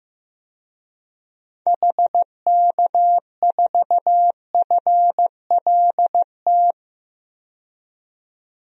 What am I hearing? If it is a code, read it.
Morse code, heard as a keyed tone, about 15 wpm, 700 Hz. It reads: HK4FLT
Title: DXCC-100 Call Signs - 6 Characters - 15wpm